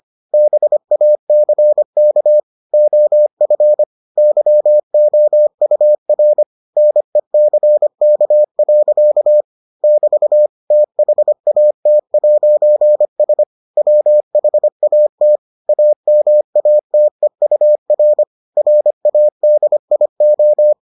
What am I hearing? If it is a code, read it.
Morse code, heard as a keyed tone, about 25 wpm, 600 Hz. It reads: BACK OF YOUR NECK. <BT> THAT'S WHAT AMATEUR RADIO